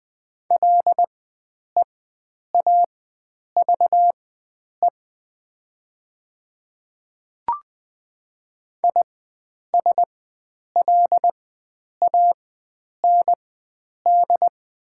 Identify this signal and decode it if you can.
Morse code, heard as a keyed tone, about 20 wpm, spaced as for 9 wpm, 700 Hz. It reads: LEAVE ISLAND
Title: 4x Character Spacing - Top 500 Words - 20wpm